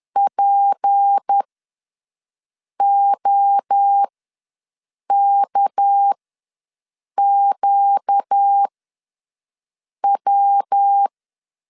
Morse code, heard as a keyed tone, about 10 wpm, spaced as for 5 wpm, 800 Hz. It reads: POKQW
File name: LEZIONI / 6 PKX0 / 6 PK ws 10wpm 3745